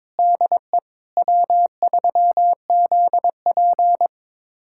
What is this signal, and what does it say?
Morse code, heard as a keyed tone, about 22 wpm, 700 Hz. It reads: DE W3ZP
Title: USA Callsign DE Callsign - 4 Characters - 22wpm